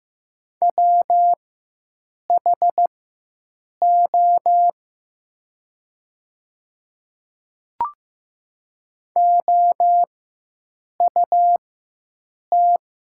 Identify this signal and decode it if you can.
Morse code, heard as a keyed tone, about 15 wpm, spaced as for 7 wpm, 700 Hz. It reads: WHO OUT
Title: 4x Character Spacing - Top 100 Words - 15wpm